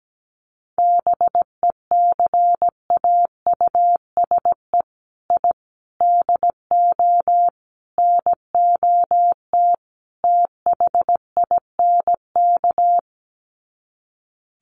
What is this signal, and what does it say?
Morse code, heard as a keyed tone, about 17 wpm, 700 Hz. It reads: BECAUSE I DO NOT THINK
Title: Sets of 5 Words Encore 17wpm